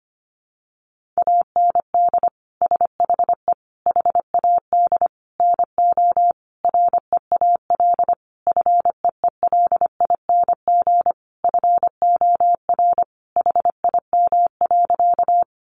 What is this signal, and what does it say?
Morse code, heard as a keyed tone, about 25 wpm, 700 Hz. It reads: AND SHE HAD NO REAL FEELING FOR HIM.
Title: Sentences from Top 1200 Words 25wpm